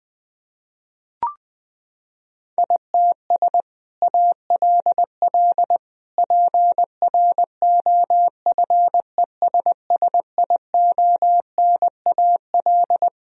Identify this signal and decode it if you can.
Morse code, heard as a keyed tone, about 21 wpm, 700 Hz. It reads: ITS ALL PROFESSIONAL
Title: Sentences from Top 1000 Words - Review Part 4 - 20wpm